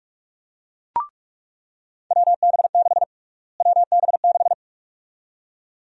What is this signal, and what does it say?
Morse code, heard as a keyed tone, about 46 wpm, 700 Hz. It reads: WB6 WB6
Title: USA Call Sign Prefixes - Repeated 2x - 45wpm